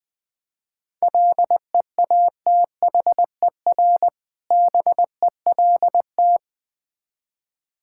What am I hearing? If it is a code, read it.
Morse code, heard as a keyed tone, about 20 wpm, 690 Hz. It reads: LEATHER BELT
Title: Difficult Words with Hints - 20wpm